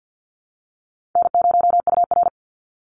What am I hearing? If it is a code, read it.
Morse code, heard as a keyed tone, about 48 wpm, 700 Hz. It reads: N0UR
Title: Easy Call Signs Repeated 2x - 50wpm